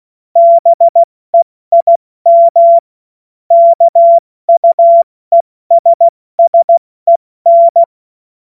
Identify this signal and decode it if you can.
Morse code, heard as a keyed tone, about 16 wpm, spaced as for 14 wpm, 680 Hz. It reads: BEIM KUESSEN